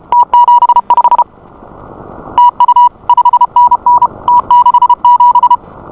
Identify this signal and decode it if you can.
Morse code, heard as a keyed tone, about 32 wpm, 970 Hz. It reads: T75 TU 5NN T67